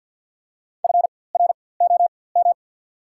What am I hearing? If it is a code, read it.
Morse code, heard as a keyed tone, about 54 wpm, spaced as for 24 wpm, 700 Hz. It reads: FROM
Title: 4x Character Spacing - Top 300 Words - 50wpm